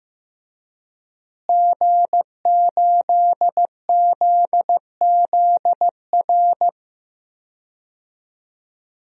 Morse code, heard as a keyed tone, about 15 wpm, 700 Hz. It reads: G8ZZR
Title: DXCC-100 Call Signs - 5 Characters - Repeated 3x - 15wpm